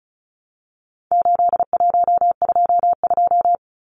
Morse code, heard as a keyed tone, about 34 wpm, 700 Hz. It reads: 8122
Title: Numbers - 4-Digit 35wpm